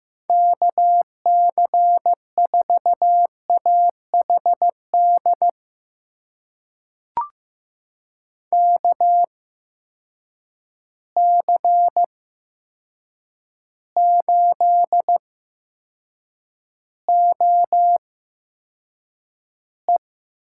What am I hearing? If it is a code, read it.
Morse code, heard as a keyed tone, about 15 wpm, spaced as for 7 wpm, 700 Hz. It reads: KC4AHD KC8OE
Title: USA Call Signs - 6 Characters - 8x Character Spacing - 15wpm